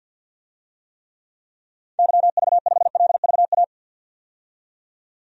Morse code, heard as a keyed tone, about 51 wpm, 700 Hz. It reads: YV5LVA
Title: DXCC-100 Call Signs - 6 Characters - 50wpm